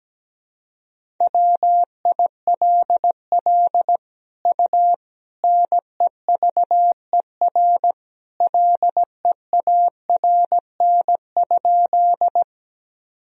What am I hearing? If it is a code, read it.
Morse code, heard as a keyed tone, about 17 wpm, 700 Hz. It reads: WILL U NEVER LEARN?
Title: Sentences from Top 300 Words and Abbreviations 17wpm